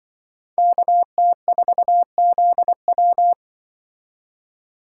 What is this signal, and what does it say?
Morse code, heard as a keyed tone, about 24 wpm, 700 Hz. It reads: KT4ZW
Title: Speed Racing - USA 2x2 Call Signs 20wpm